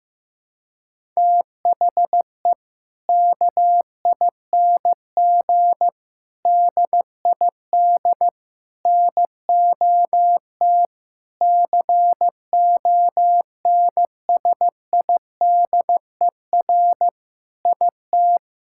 Morse code, heard as a keyed tone, about 16 wpm, 690 Hz. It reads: THE KING DID NOT CONSIDER IT
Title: Sentences from Top 1000 Words - Review Part 5 - 15wpm